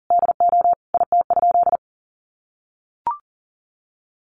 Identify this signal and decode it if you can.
Morse code, heard as a keyed tone, about 37 wpm, 700 Hz. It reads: DO IT?
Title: Sentences from Top 1200 Words 40wpm